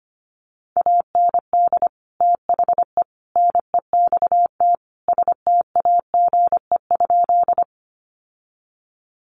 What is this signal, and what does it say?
Morse code, heard as a keyed tone, about 25 wpm, 700 Hz. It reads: AND THE NEXT STAGE?
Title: Sentences from Top 1200 Words 25wpm